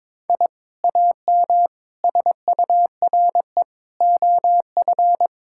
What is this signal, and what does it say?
Morse code, heard as a keyed tone, about 22 wpm, 700 Hz. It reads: I AM SURE OF
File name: practice / 22wpm / Sentences with 2-Word Phrases - 22wpm